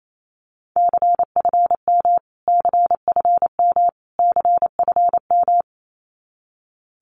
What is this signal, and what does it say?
Morse code, heard as a keyed tone, about 28 wpm, 700 Hz. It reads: CFM CFM CFM
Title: Abbreviations Repeated 3x - 28wpm